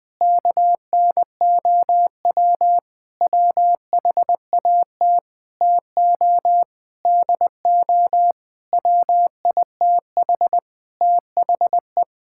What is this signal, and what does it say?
Morse code, heard as a keyed tone, about 20 wpm, 700 Hz. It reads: KNOW WHAT TO DO WITH THE
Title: Sentences with 5-Word Phrases - 20wpm